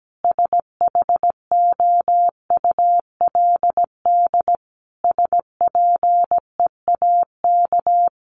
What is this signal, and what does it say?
Morse code, heard as a keyed tone, about 17 wpm, 700 Hz. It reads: SHOULD SPEAK
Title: Sentences from Top 1100 Words 17wpm